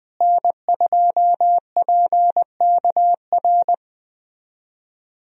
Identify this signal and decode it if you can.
Morse code, heard as a keyed tone, about 20 wpm, 700 Hz. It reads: N2PKR
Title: USA Call Signs - 5 Characters - Repeated 3x - 20wpm